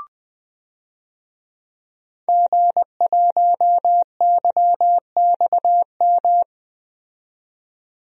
Morse code, heard as a keyed tone, about 20 wpm, 700 Hz. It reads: G1YXM
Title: DXCC-100 Call Signs - 5 Characters - 20wpm